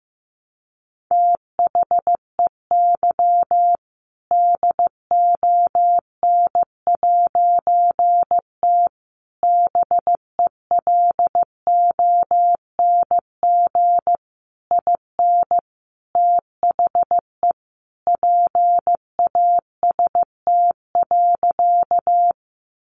Morse code, heard as a keyed tone, about 15 wpm, 700 Hz. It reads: THEY DON'T BELONG IN THE PAST.
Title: A Sound Of Thunder 15wpm